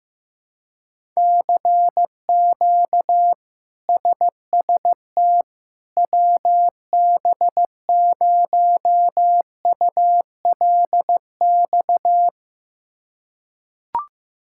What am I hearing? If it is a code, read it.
Morse code, heard as a keyed tone, about 15 wpm, 700 Hz. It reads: CQ SST WB0ULX
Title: K1USN SST QSO Practice - 15wpm